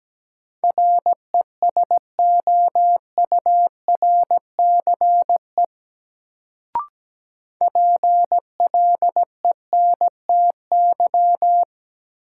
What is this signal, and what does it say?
Morse code, heard as a keyed tone, about 17 wpm, 700 Hz. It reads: RESOURCE PLENTY
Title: Sentences from Top 1300 Words - New Word Drill Alt - 17wpm